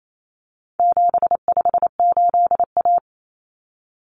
Morse code, heard as a keyed tone, about 28 wpm, 700 Hz. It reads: 758A
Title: Common Cut Numbers - 4-Digit 28wpm